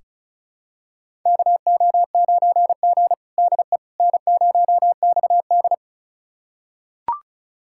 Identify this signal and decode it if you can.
Morse code, heard as a keyed tone, about 36 wpm, 700 Hz. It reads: KO9G DE N0XD
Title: USA Callsign DE Callsign - 4 Characters - 35wpm